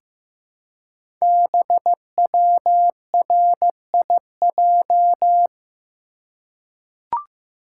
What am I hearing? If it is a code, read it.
Morse code, heard as a keyed tone, about 15 wpm, 700 Hz. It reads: BWRIJ
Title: Code Groups - 5 Letters - 15wpm